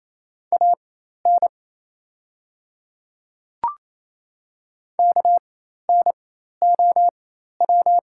Morse code, heard as a keyed tone, about 28 wpm, spaced as for 13 wpm, 700 Hz. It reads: AN KNOW
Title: 4x Character Spacing - Top 100 Words - 28wpm